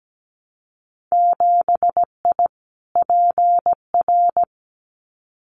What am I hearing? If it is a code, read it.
Morse code, heard as a keyed tone, about 17 wpm, 700 Hz. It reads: 7I PR
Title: Winter Field Day Exchanges Encore 17wpm